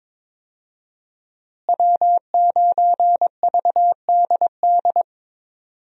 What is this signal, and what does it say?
Morse code, heard as a keyed tone, about 22 wpm, 700 Hz. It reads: W9VDD
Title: USA Call Signs - 5 Characters - Repeated 2x - 22wpm